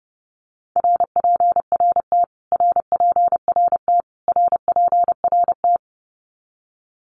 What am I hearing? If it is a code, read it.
Morse code, heard as a keyed tone, about 30 wpm, 700 Hz. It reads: RPRT RPRT RPRT
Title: Abbreviations Repeated 3x - 30wpm